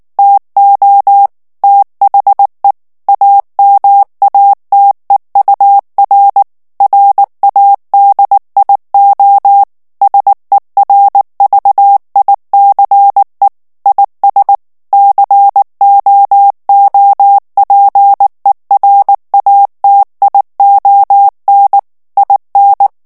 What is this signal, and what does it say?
Morse code, heard as a keyed tone, about 19 wpm, 800 Hz. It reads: TO THE AMATEUR RADIO SERVICE IS COOPERATION IN
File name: ANGLAIS texte clair 3 vitesse 18 mots minute